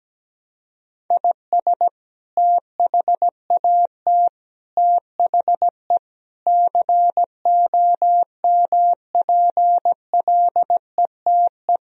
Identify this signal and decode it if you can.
Morse code, heard as a keyed tone, about 17 wpm, 700 Hz. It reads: IS THAT THE COMPLETE
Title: Sentences from Top 1000 Words - Review Part 2 - 17wpm